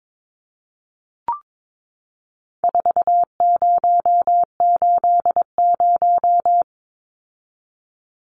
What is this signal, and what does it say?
Morse code, heard as a keyed tone, about 22 wpm, 700 Hz. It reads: 4080
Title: Numbers - 4-Digit 22wpm